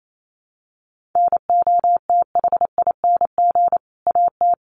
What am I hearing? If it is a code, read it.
Morse code, heard as a keyed tone, about 28 wpm, 700 Hz. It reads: NOTHING AT
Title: Sentences from Top 500 Words 28wpm